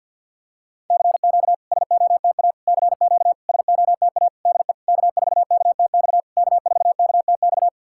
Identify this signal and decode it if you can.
Morse code, heard as a keyed tone, about 52 wpm, 690 Hz. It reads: CQ SOTA CQ SOTA DE K4KTX K4KTX
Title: SOTA QSO Practice 50wpm